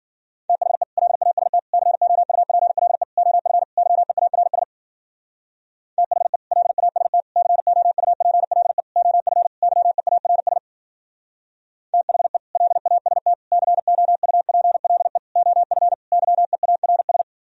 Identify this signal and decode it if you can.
Morse code, heard as a keyed tone, about 52 wpm, 700 Hz. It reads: THE LAST COUPLE OF YEARS THE LAST COUPLE OF YEARS THE LAST COUPLE OF YEARS